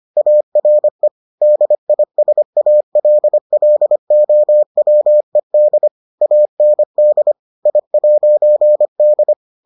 Morse code, heard as a keyed tone, about 25 wpm, 600 Hz. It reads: ARE DISALLOWED AND I'D